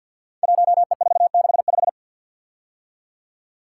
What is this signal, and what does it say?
Morse code, heard as a keyed tone, about 50 wpm, 700 Hz. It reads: 1E465
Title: Code Groups - 5 Letter:Numbers - 50wpm